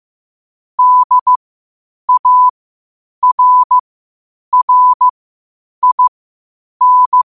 Morse code, heard as a keyed tone, about 15 wpm, spaced as for 8 wpm, 1000 Hz. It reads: DARRIN
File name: qotdfiles / 8wpm